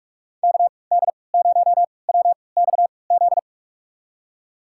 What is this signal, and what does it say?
Morse code, heard as a keyed tone, about 43 wpm, spaced as for 25 wpm, 700 Hz. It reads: KD0WXZ
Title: USA Call Signs - 6 Characters - 3x Character Spacing - 45wpm